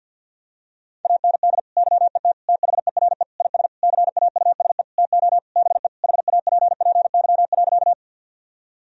Morse code, heard as a keyed tone, about 49 wpm, 700 Hz. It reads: AND YET THERE IS CAUSE TO BE HAPPY.
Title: Sentences from Top 1000 Words - Review Part 1 - 50wpm